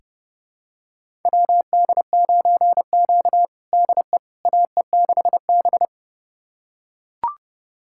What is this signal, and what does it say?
Morse code, heard as a keyed tone, about 32 wpm, 700 Hz. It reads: WD9Q DE AE6B
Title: USA Callsign DE Callsign - 4 Characters - 30wpm